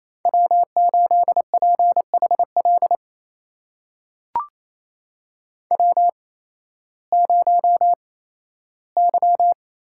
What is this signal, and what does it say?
Morse code, heard as a keyed tone, about 28 wpm, spaced as for 15 wpm, 700 Hz. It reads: W8PHL W0Y
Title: USA Call Signs - 8x Character Spacing - 28wpm